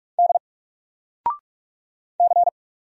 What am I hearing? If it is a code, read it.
Morse code, heard as a keyed tone, about 44 wpm, 700 Hz. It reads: D C
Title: Single Letter-Number Alt 45wpm